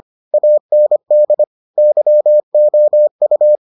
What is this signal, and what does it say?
Morse code, heard as a keyed tone, about 25 wpm, 600 Hz. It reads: AND YOU